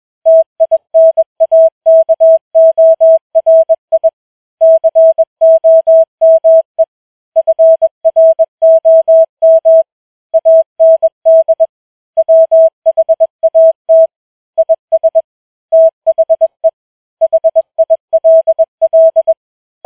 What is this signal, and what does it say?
Morse code, heard as a keyed tone, about 21 wpm, 650 Hz. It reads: TINAKORI COME FROM AND WHAT IS THE HILL